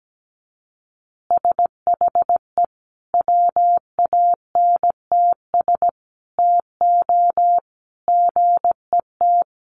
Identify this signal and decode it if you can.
Morse code, heard as a keyed tone, about 17 wpm, 700 Hz. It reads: SHE WANTS TO GET
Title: Easy Phrases 17wpm